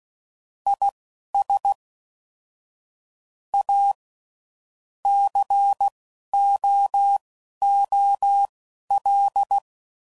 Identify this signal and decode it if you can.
Morse code, heard as a keyed tone, about 16 wpm, spaced as for 11 wpm, 800 Hz. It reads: IS A COOL
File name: CW Code Groups @10WPM / Weather Sets / Weather 4 14